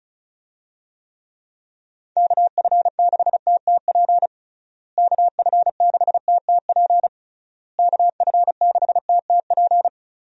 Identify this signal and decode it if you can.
Morse code, heard as a keyed tone, about 36 wpm, 700 Hz. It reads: KF6TTP KF6TTP KF6TTP
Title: USA Call Signs - 6 Characters - Repeated 3x - 35wpm